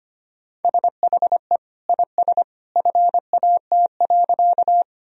Morse code, heard as a keyed tone, about 25 wpm, 700 Hz. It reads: SHE IS FAT.
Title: Sentences from Top 1000 Words - Review Part 4 - 25wpm